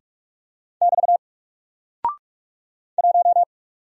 Morse code, heard as a keyed tone, about 44 wpm, 700 Hz. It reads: <BT> 1
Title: Single Character (Letter:Number:Punctuation:Prosign) 45wpm